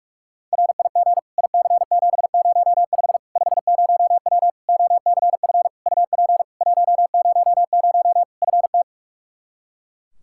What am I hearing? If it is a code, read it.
Morse code, heard as a keyed tone, about 46 wpm, 700 Hz. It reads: RIG IC705 50W OCF UP 100 FT
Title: Practice QSO - Exchange 2 - Info - 45wpm